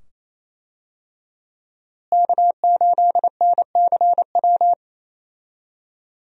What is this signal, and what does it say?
Morse code, heard as a keyed tone, about 28 wpm, 700 Hz. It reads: K8NCW
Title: USA Call Signs - 5 Characters - Repeated 2x - 28wpm